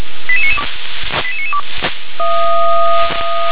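Five-tone selective call (ZVEI), heard as quick runs of five tones, tones of about 70 ms. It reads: 88992 88992
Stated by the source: ZVEI five-tone selective call received by SDR.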